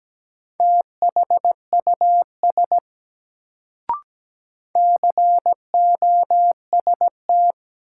Morse code, heard as a keyed tone, about 17 wpm, 700 Hz. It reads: THUS COST
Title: Sentences from Top 800 Words - New Word Drill - 17wpm